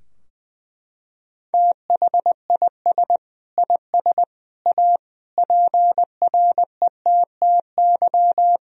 Morse code, heard as a keyed tone, about 20 wpm, 700 Hz. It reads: THIS IS A PRETTY
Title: Sentences from Top 1000 Words - Review Part 2 - 20wpm